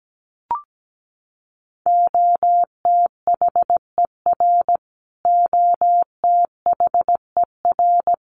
Morse code, heard as a keyed tone, about 17 wpm, 700 Hz. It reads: OTHER OTHER
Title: Top 200 Words Repeated 17wpm